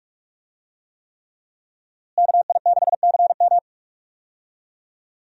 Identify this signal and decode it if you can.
Morse code, heard as a keyed tone, about 46 wpm, 700 Hz. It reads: KI6CM
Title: USA Call Signs - 5 Characters - Repeated 3x - 45wpm